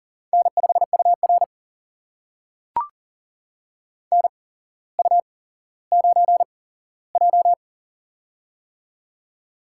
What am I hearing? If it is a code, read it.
Morse code, heard as a keyed tone, about 40 wpm, spaced as for 17 wpm, 700 Hz. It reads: N5UR NU9J
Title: USA Call Signs - 4 Characters - 8x Character Spacing - 40wpm